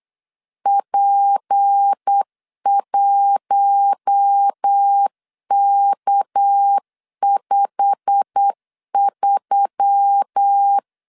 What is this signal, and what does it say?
Morse code, heard as a keyed tone, about 8 wpm, 800 Hz. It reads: P1K53
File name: LEZIONI / 10 SIGNS / 10D COMA 8wpm 3749